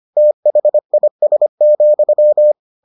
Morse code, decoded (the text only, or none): THIS,